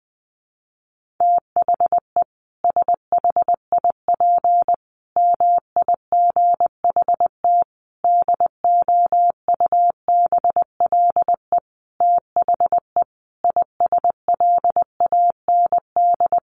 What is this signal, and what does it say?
Morse code, heard as a keyed tone, about 20 wpm, 700 Hz. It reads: THE SHIP MIGHT DOUBLE THE ISLAND
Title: Sentences from Top 1400 Words 20wpm